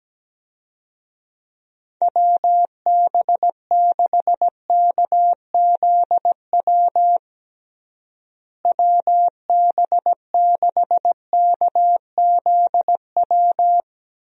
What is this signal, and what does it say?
Morse code, heard as a keyed tone, about 17 wpm, 700 Hz. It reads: WB6KZW WB6KZW
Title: USA Call Signs - 6 Characters - Repeated 3x - 17wpm